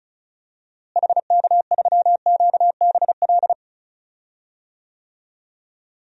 Morse code, heard as a keyed tone, about 36 wpm, 700 Hz. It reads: HK3QBL
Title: DXCC-100 Call Signs - 6 Characters - Repeated 2x - 35wpm